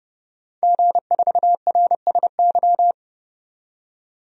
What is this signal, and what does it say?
Morse code, heard as a keyed tone, about 32 wpm, 700 Hz. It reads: G4RSY
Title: DXCC-100 Call Signs - 5 Characters - 30wpm